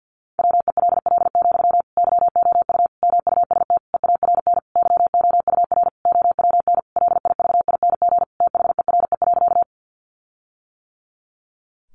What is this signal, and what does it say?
Morse code, heard as a keyed tone, about 50 wpm, 700 Hz. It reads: WELL, YOU MUST EARN YOUR OWN LIVING THERE.
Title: Sentences from Top 1500 Words 50wpm